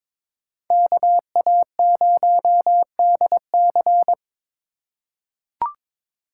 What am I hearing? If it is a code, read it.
Morse code, heard as a keyed tone, about 23 wpm, 700 Hz. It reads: KA0DC
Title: USA Call Signs - 5 Characters - 2x Character Spacing - 22wpm